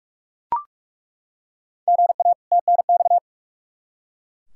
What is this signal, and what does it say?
Morse code, heard as a keyed tone, about 42 wpm, 700 Hz. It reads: GA TNX
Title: K1USN SST QSO Practice 45wpm